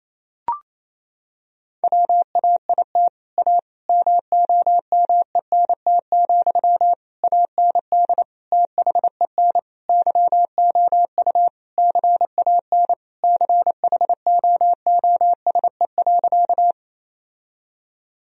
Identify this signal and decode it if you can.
Morse code, heard as a keyed tone, about 28 wpm, 700 Hz. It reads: WAIT A MOMENT, AND THEN YOU CAN CHOOSE.
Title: Sentences from Top 1000 Words - Review Part 4 - 28wpm